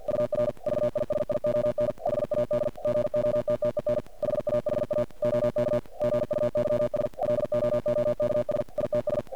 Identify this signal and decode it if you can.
Morse code, heard as a keyed tone, about 48 wpm, 590 Hz. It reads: UR VISION HAD GOTTEN HAHA OK GUYS LOOKS ITS
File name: archived / CRSnet